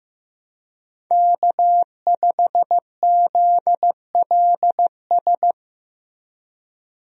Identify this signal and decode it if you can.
Morse code, heard as a keyed tone, about 15 wpm, 700 Hz. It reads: K5ZLS
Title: USA Call Signs - 5 Characters - Repeated 3x - 15wpm